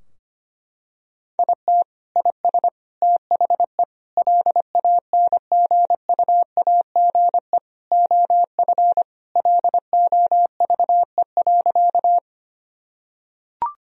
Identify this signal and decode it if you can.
Morse code, heard as a keyed tone, about 25 wpm, 700 Hz. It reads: IT IS THE LANGUAGE OF LOVE.
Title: Sentences from Top 1000 Words - Review Part 3 - 25wpm